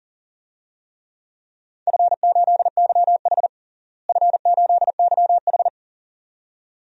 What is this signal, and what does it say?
Morse code, heard as a keyed tone, about 37 wpm, 700 Hz. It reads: F8YH F8YH
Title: DXCC-100 Call Signs - 4 Characters - Repeated 2x - 40wpm